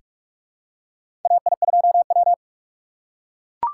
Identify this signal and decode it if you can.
Morse code, heard as a keyed tone, about 42 wpm, 700 Hz. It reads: AI2W
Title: USA Call Signs - 4 Characters - 4x Character Spacing - 45wpm